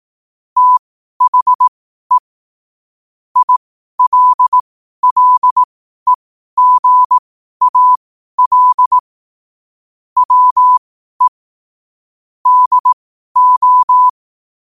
Morse code, heard as a keyed tone, about 18 wpm, spaced as for 12 wpm, 1000 Hz. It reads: THE ILLEGAL WE DO
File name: qotdfiles / 12wpm